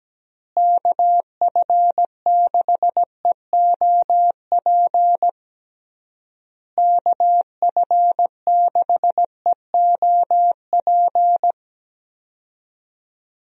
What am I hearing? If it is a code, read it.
Morse code, heard as a keyed tone, about 17 wpm, 700 Hz. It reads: KF6EOP KF6EOP
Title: USA Call Signs - 6 Characters - Repeated 3x - 17wpm